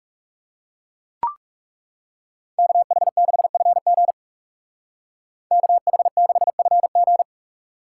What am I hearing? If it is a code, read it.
Morse code, heard as a keyed tone, about 35 wpm, 700 Hz. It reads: KH6FG KH6FG
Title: Speed Racing - USA 2x2 Call Signs 30wpm